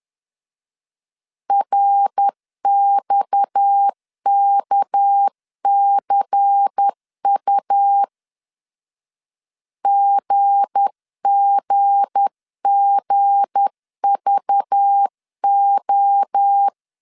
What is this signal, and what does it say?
Morse code, heard as a keyed tone, about 10 wpm, 800 Hz. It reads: RXKCU GGGVO
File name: LEZIONI / 7 GF9Z / 7 G 10wpm 2501